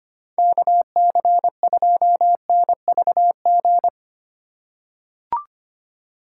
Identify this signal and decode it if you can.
Morse code, heard as a keyed tone, about 25 wpm, 700 Hz. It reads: KC2NVG
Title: USA Call Signs - 6 Characters - 2x Character Spacing - 25wpm